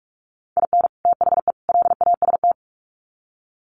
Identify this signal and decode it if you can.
Morse code, heard as a keyed tone, about 44 wpm, 700 Hz. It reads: IN THE LAST
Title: Sets of 3 Words 45wpm